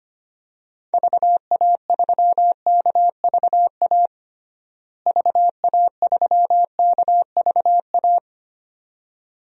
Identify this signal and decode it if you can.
Morse code, heard as a keyed tone, about 25 wpm, 700 Hz. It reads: VA3KVA VA3KVA
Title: DXCC-100 Call Signs - 6 Characters - Repeated 2x - 25wpm